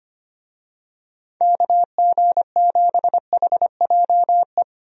Morse code, heard as a keyed tone, about 25 wpm, 700 Hz. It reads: KG7HJE
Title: USA Call Signs - 6 Characters - Repeated 3x - 25wpm